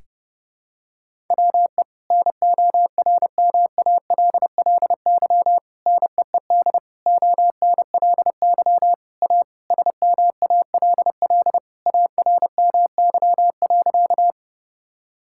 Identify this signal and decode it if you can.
Morse code, heard as a keyed tone, about 30 wpm, 700 Hz. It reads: WE NORMALLY NEED ONLY A SMALL ARMY.
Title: Sentences from Top 1000 Words - Review Part 6 - 30wpm